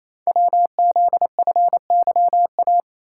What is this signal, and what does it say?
Morse code, heard as a keyed tone, about 27 wpm, 700 Hz. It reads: WZFYA
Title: Code Groups - 5 Letters - 28wpm